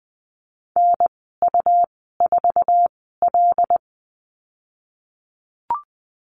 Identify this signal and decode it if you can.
Morse code, heard as a keyed tone, about 20 wpm, spaced as for 15 wpm, 700 Hz. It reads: NU4L
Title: 167 Lesson 27g - Number 7 - Call Signs 20wpm